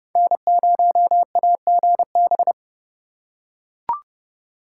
Morse code, heard as a keyed tone, about 28 wpm, 700 Hz. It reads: N0AGB